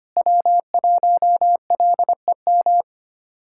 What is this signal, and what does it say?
Morse code, heard as a keyed tone, about 25 wpm, 700 Hz. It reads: W1LEM
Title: K1USN SST QSO Practice - 25wpm